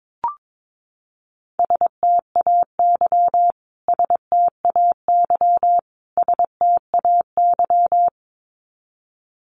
Morse code, heard as a keyed tone, about 22 wpm, 700 Hz. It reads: STAY STAY STAY
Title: Top 500 Words Repeated 22wpm